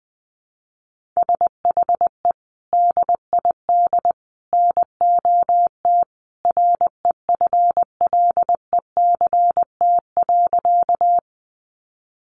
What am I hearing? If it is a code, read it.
Morse code, heard as a keyed tone, about 20 wpm, 700 Hz. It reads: SHE DID NOT REFLECT.
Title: Sentences from Top 1500 Words 20wpm